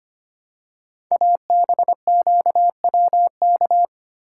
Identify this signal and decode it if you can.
Morse code, heard as a keyed tone, about 25 wpm, 700 Hz. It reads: ABQWK
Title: Code Groups - 5 Letters - 25wpm